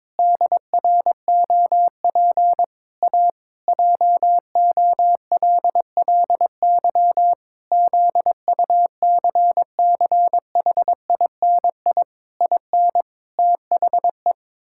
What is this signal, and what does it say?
Morse code, heard as a keyed tone, about 22 wpm, 700 Hz. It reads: DROP A JOLLY ZUCCHINI IN THE
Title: Pangram Challenge v2 - 22wpm